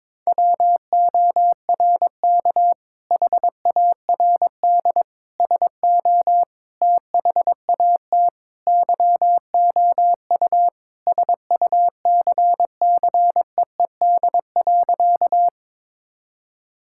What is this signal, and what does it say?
Morse code, heard as a keyed tone, about 22 wpm, 700 Hz. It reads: WORK HARD SO THAT YOU SUCCEED.
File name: practice / 22wpm / Sentences with 2-Word Phrases - 22wpm